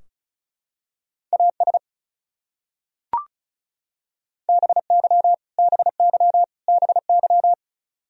Morse code, heard as a keyed tone, about 34 wpm, 700 Hz. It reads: AS BY BY BY
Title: Top 100 Words Repeated 35wpm